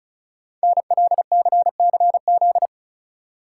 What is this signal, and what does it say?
Morse code, heard as a keyed tone, about 36 wpm, 690 Hz. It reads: NLCCZ
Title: Code Groups - 5 Letters - 35wpm